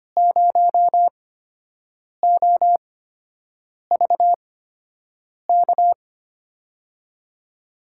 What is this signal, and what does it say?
Morse code, heard as a keyed tone, about 25 wpm, spaced as for 7 wpm, 700 Hz. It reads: 0OVK
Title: USA Call Signs - 5 Characters - 8x Character Spacing - 25wpm